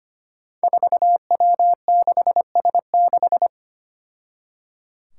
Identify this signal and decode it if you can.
Morse code, heard as a keyed tone, about 25 wpm, 700 Hz. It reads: 4W6S6
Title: Five Challenging Letter-Number Combinations 25wpm